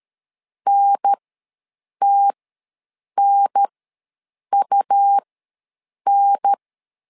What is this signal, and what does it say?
Morse code, heard as a keyed tone, about 13 wpm, spaced as for 7 wpm, 800 Hz. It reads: NTNUN
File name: LEZIONI / 2 UHDN / 2 UHDN ws 12wpm 1708